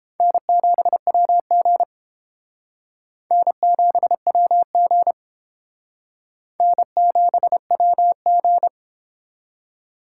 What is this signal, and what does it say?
Morse code, heard as a keyed tone, about 28 wpm, 700 Hz. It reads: N7WG N7WG N7WG